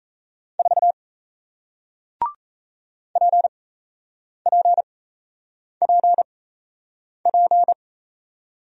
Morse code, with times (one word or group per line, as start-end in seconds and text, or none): 0.6-0.9 4
3.2-7.7 PPPP